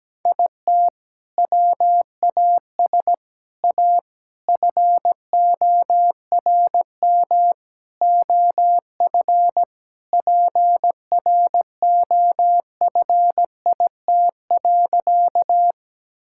Morse code, heard as a keyed tone, about 17 wpm, 700 Hz. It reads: IT WAS A FORM OF PROFIT.
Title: Sentences from Top 1000 Words - Review Part 5 - 17wpm